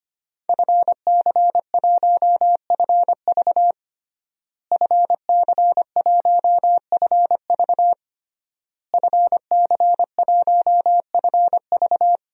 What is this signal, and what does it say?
Morse code, heard as a keyed tone, about 25 wpm, 700 Hz. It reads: FC1FV FC1FV FC1FV
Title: DXCC-100 Call Signs - 5 Characters - Repeated 3x - 25wpm